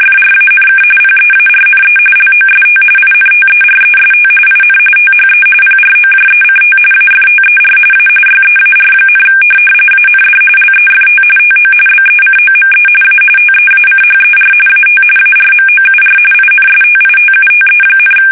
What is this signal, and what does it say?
RTTY, about 45.45 baud. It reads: FRANK PEARNE LOST INTEREST IN THE PROJECT AFTER A YEAR, AND LEFT TO BECOME A TEACHER AT ARMOUR INSTITUTE,
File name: fsk850 / teletype-corp